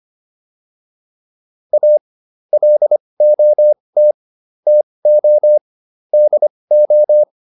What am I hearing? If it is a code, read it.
Morse code, heard as a keyed tone, about 25 wpm, spaced as for 20 wpm, 600 Hz. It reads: A LOT TO DO